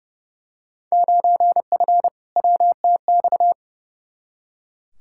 Morse code, heard as a keyed tone, about 32 wpm, 700 Hz. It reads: 9F WTX
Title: Field Day Exchanges Encore - 30wpm